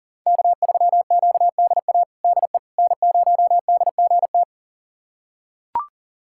Morse code, heard as a keyed tone, about 40 wpm, 700 Hz. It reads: K3QDA DE N0DGT
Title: USA Callsign DE Callsign - 5 Characters - 40wpm